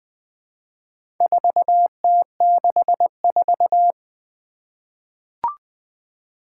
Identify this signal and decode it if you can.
Morse code, heard as a keyed tone, about 20 wpm, 700 Hz. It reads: 4T64
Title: Four Challenging Letter-Number Combinations 20wpm